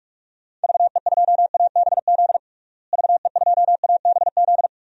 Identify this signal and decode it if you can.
Morse code, heard as a keyed tone, about 46 wpm, 700 Hz. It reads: VE2ABZ VE2ABZ
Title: DXCC-100 Call Signs - 6 Characters - Repeated 2x - 45wpm